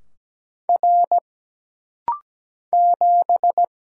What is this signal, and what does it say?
Morse code, heard as a keyed tone, about 17 wpm, 700 Hz. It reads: R 7
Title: Single Letter-Number - Rapid-Fire 17wpm